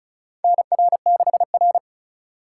Morse code, heard as a keyed tone, about 36 wpm, 700 Hz. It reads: NR6R
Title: USA Call Signs - 4 Characters - 2x Character Spacing - 35wpm